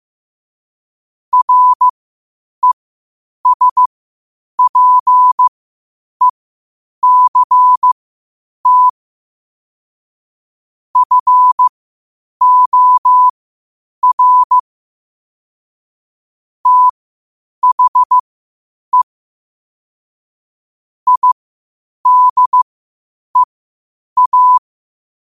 Morse code, heard as a keyed tone, about 15 wpm, spaced as for 8 wpm, 1000 Hz. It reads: RESPECT FOR THE IDEA
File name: qotdfiles / 8wpm